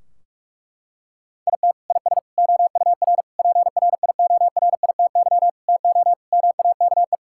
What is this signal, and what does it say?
Morse code, heard as a keyed tone, about 46 wpm, 700 Hz. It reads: IT IS OUR PRIORITY TO MAKE
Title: Sentences from Top 1600 Words 45wpm